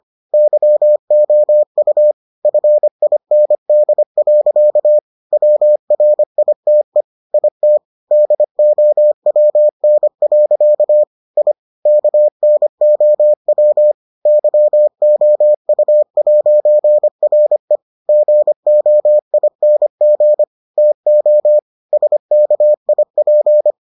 Morse code, heard as a keyed tone, about 25 wpm, 600 Hz. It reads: YOU FIND. WRITE IT DOWN. I KNOW YOU'RE GOING TO SKIP